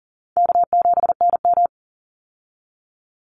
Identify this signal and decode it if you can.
Morse code, heard as a keyed tone, about 40 wpm, 700 Hz. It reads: K7NM
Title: Easy Call Signs - 40wpm